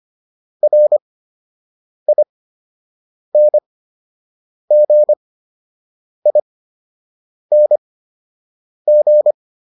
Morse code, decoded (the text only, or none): RINGING